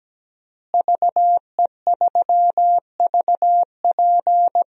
Morse code, heard as a keyed tone, about 17 wpm, 700 Hz. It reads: VE3VP
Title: DXCC-100 Call Signs - 5 Characters - Repeated 2x - 17wpm